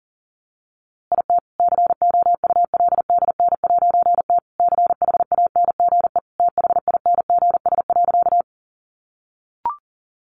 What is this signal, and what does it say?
Morse code, heard as a keyed tone, about 40 wpm, 700 Hz. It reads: IT COULDN'T CHANGE THINGS.
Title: A Sound Of Thunder 40wpm